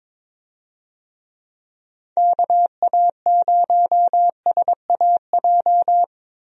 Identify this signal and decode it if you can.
Morse code, heard as a keyed tone, about 22 wpm, 700 Hz. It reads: KA0SAJ
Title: USA Call Signs - 6 Characters - Repeated 3x - 22wpm